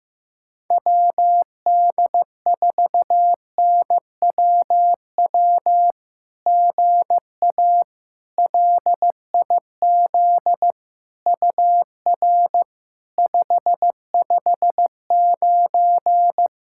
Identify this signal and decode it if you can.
Morse code, heard as a keyed tone, about 15 wpm, 700 Hz. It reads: WD4NWW GA LIZ UR 559
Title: SOTA QSO Practice 15wpm